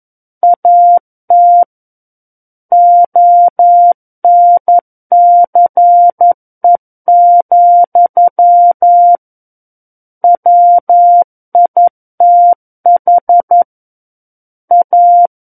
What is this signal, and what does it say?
Morse code, heard as a keyed tone, about 11 wpm, 700 Hz. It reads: AT ONCE, WITH A
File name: 11 WPM